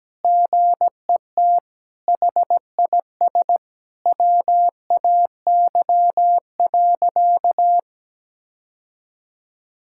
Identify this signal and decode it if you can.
Morse code, heard as a keyed tone, about 17 wpm, 700 Hz. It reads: GET HIS WAY.
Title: Sentences from Top 1000 Words - Review Part 6 - 17wpm